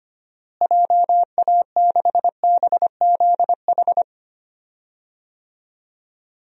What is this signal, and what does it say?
Morse code, heard as a keyed tone, about 25 wpm, 700 Hz. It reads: JA6BZH
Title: DXCC-100 Call Signs - 6 Characters - Repeated 2x - 25wpm